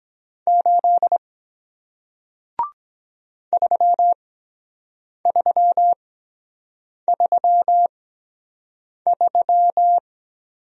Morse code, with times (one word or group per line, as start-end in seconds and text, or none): 0.5-1.2 8
3.5-10.0 3333